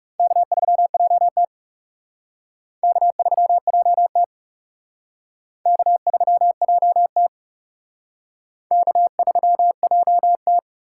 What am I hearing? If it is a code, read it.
Morse code, heard as a keyed tone, about 37 wpm, 700 Hz. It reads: K3JT K3JT K3JT K3JT
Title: Speed Racing - USA 1x2 Call Signs 30wpm